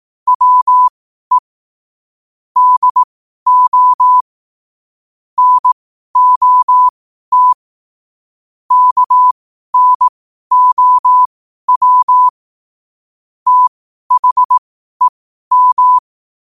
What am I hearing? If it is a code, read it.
Morse code, heard as a keyed tone, about 18 wpm, spaced as for 12 wpm, 1000 Hz. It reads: WE DO NOT KNOW THEM
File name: qotdfiles / 12wpm